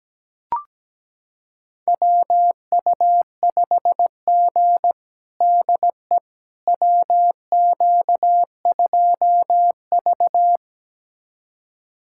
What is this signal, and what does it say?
Morse code, heard as a keyed tone, about 17 wpm, 700 Hz. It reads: WU5G DE WQ2V
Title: USA Callsign DE Callsign - 4 Characters - 17wpm